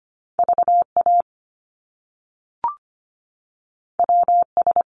Morse code, heard as a keyed tone, about 25 wpm, 700 Hz. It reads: VA WS
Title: Top Two Letter Combinations in Words 25wpm